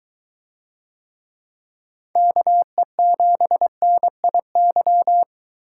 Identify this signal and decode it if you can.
Morse code, heard as a keyed tone, about 23 wpm, 700 Hz. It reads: KE7NIY